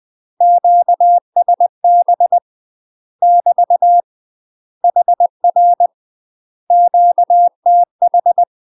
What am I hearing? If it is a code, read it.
Morse code, heard as a keyed tone, about 20 wpm, 700 Hz. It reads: QSB <BT> HR QTH
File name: UK MASTER QSOs TRAINING / 20WPM / UK MASTER QSOs 0052